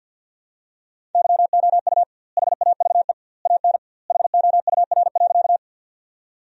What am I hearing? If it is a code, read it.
Morse code, heard as a keyed tone, about 50 wpm, 700 Hz. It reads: YOU HAVE AN HOUR.
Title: 2x Word-Spacing - Sentences from Top 400 Words - 50wpm